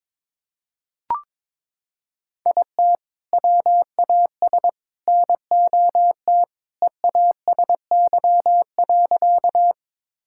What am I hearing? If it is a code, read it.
Morse code, heard as a keyed tone, about 21 wpm, 700 Hz. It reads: IT WAS NOT EASY.
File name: practice / 22wpm / Sentences with 3-Word Phrases - 22wpm